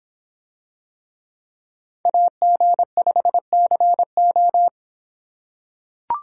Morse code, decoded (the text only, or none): AG5CO